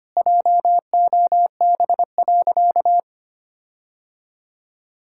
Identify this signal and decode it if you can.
Morse code, heard as a keyed tone, about 25 wpm, 700 Hz. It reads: JOB.
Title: Sentences from Top 1600 Words 25wpm